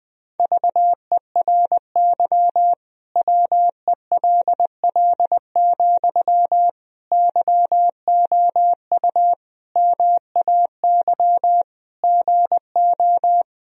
Morse code, heard as a keyed tone, about 21 wpm, 700 Hz. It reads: VERY WELL, YOU MAY GO
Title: Sentences from Top 1000 Words - Review Part 4 - 20wpm